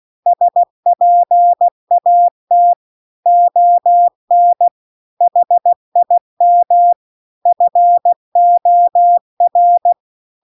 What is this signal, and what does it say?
Morse code, heard as a keyed tone, about 16 wpm, 700 Hz. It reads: SPAT ON HIM FOR
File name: CW-The-Big-Twoheaded-River-2-16WPM